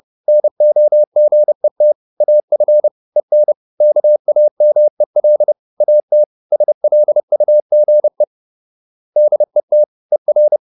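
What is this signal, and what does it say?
Morse code, heard as a keyed tone, about 30 wpm, 600 Hz. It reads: NOGET AF EN KAMEL AT SLUGE DET ER
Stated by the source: Fast (QRQ) Morse at 29 wpm, from CW med Gnister.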